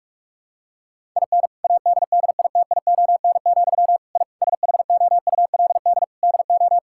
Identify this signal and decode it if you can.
Morse code, heard as a keyed tone, about 46 wpm, 700 Hz. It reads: IN ADDITION, I SHOULD DO
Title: Sentences from Top 1000 Words - Review Part 6 - 45wpm